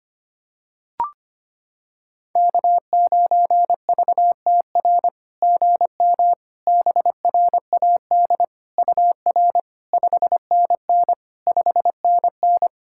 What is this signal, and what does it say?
Morse code, heard as a keyed tone, about 25 wpm, 700 Hz. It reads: K9VTR GM BRAD UR 5NN 5NN
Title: SOTA QSO Practice 25wpm